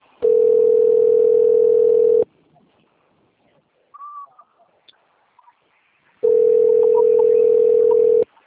Ringback tone (North America), 440 + 480 Hz.
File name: rg-1600-unknown- - - .20073